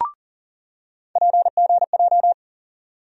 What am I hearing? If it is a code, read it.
Morse code, heard as a keyed tone, about 40 wpm, 700 Hz. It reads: PGJ
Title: USA Call Sign Postfixes - 3 Letters 40wpm